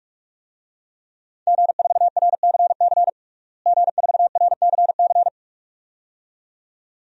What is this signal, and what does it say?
Morse code, heard as a keyed tone, about 45 wpm, 700 Hz. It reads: G4RCC G4RCC
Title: DXCC-100 Call Signs - 5 Characters - Repeated 2x - 45wpm